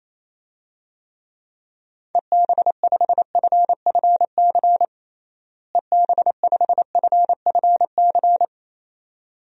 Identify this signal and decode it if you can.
Morse code, heard as a keyed tone, about 28 wpm, 700 Hz. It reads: EB5FFC EB5FFC
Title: DXCC-100 Call Signs - 6 Characters - Repeated 2x - 28wpm